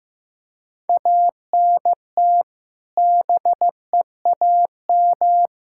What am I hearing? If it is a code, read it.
Morse code, heard as a keyed tone, about 15 wpm, 700 Hz. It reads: ANT BEAM
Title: K1USN SST QSO Practice 15wpm